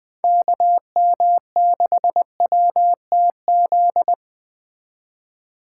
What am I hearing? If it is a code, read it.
Morse code, heard as a keyed tone, about 20 wpm, 700 Hz. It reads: KM6WTZ
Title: Speed Racing - USA Call Signs 17wpm